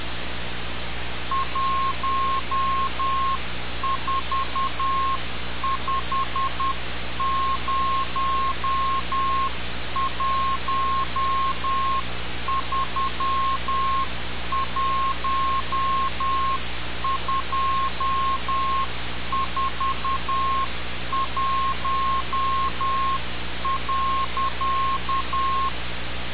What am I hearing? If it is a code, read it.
Morse code, heard as a keyed tone, about 10 wpm, 1070 Hz. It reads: 1450131241.
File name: radiocap / REC z